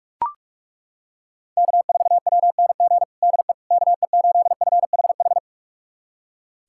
Morse code, heard as a keyed tone, about 42 wpm, 700 Hz. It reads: K4WNG DE KE8FHH
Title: USA Callsign DE Callsign 45wpm